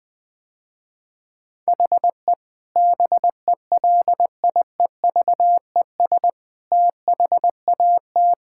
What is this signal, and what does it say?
Morse code, heard as a keyed tone, about 21 wpm, 700 Hz. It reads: HE BELIEVES THAT
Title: Sentences with 9-Word Phrases - Spoken Beginning - 20wpm